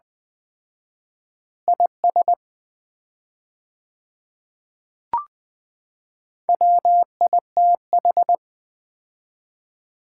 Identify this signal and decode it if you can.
Morse code, heard as a keyed tone, about 20 wpm, 700 Hz. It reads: IS WITH
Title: 4x Word-Spacing - Sets of 2 Words - 20wpm